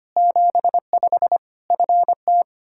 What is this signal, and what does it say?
Morse code, heard as a keyed tone, about 25 wpm, 700 Hz. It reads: 75 FT
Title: Practice QSO - Exchange 2 - Info - 25wpm